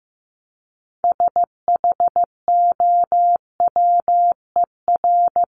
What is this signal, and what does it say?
Morse code, heard as a keyed tone, about 15 wpm, 700 Hz. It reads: SHOWER
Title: Top 1501-1750 Words 15wpm